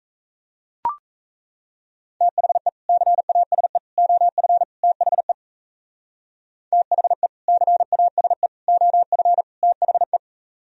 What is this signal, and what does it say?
Morse code, heard as a keyed tone, about 34 wpm, 700 Hz. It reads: THE CASE OF THE THE CASE OF THE
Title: Speed-Racing - Sets of 4 Words 28wpm